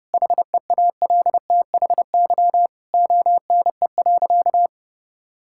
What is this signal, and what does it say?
Morse code, heard as a keyed tone, about 30 wpm, 700 Hz. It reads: HEALTHY ONE.
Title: Sentences from Top 1000 Words - Review Part 5 - 30wpm